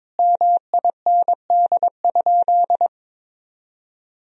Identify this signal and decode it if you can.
Morse code, heard as a keyed tone, about 22 wpm, 700 Hz. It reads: MIND?
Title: Sentences from Top 1000 Words - Review Part 2 - 22wpm